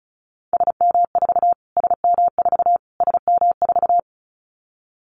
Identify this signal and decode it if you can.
Morse code, heard as a keyed tone, about 34 wpm, 700 Hz. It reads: SM4 SM4 SM4
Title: Callsign Prefixes for Top 25 DXCC - Repeated 3x - 35wpm